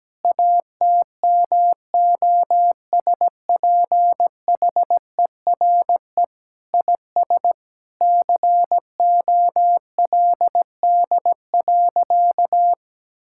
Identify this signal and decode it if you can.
Morse code, heard as a keyed tone, about 17 wpm, 700 Hz. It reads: ATMOSPHERE IS COLD.
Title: Sentences from Top 1600 Words 17wpm